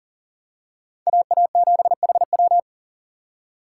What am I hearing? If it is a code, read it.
Morse code, heard as a keyed tone, about 40 wpm, 700 Hz. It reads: AA7HW
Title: USA Call Signs - 5 Characters - 4x Character Spacing - 40wpm